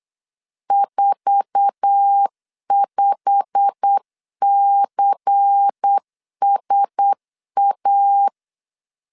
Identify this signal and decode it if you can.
Morse code, heard as a keyed tone, about 8 wpm, 800 Hz. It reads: 45CSA
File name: LEZIONI / 9 2 8wpm 3437